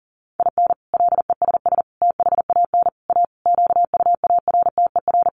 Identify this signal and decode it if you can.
Morse code, heard as a keyed tone, about 40 wpm, 700 Hz. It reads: IN LESS THAN A QUARTER
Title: Sentences from Top 1300 Words 40wpm